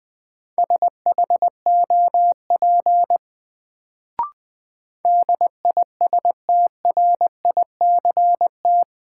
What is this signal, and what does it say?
Morse code, heard as a keyed tone, about 20 wpm, 700 Hz. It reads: SHOP DISTRICT
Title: Sentences from Top 1600 Words - New Word Drill - 20wpm